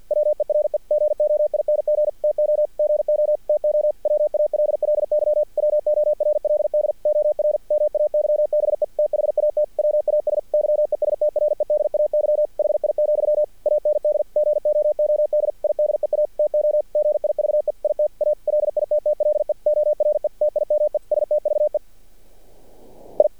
Morse code, heard as a keyed tone, about 48 wpm, 600 Hz. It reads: WERE GOING TO GO TO WALLY WORLD OR MAYBE THAT WAS YESTREDAY HI, AND GOOD IDEA TO GIVE IT A LITTLE ORE TIME STVE I
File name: archived / CRSnet